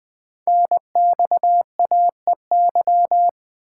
Morse code, heard as a keyed tone, about 20 wpm, 700 Hz. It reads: NXAEY
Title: Code Groups - 5 Letters - 20wpm